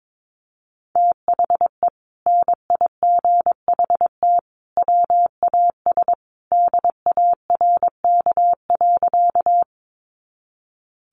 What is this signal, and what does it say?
Morse code, heard as a keyed tone, about 22 wpm, 700 Hz. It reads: THE NIGHT WAS DARK.
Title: Sentences from Top 500 Words 22wpm